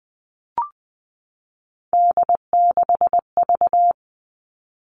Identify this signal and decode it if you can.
Morse code, heard as a keyed tone, about 20 wpm, 700 Hz. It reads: D6V
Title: Three Challenging Letter-Number Combinations 20wpm